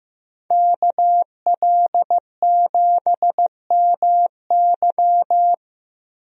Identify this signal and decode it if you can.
Morse code, heard as a keyed tone, about 15 wpm, 700 Hz. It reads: KL7MY
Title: USA Call Signs - 5 Characters - 15wpm